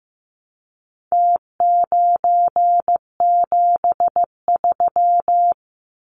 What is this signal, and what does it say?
Morse code, heard as a keyed tone, about 15 wpm, 700 Hz. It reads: T973
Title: Numbers - Contest Style - 4-Digit 15wpm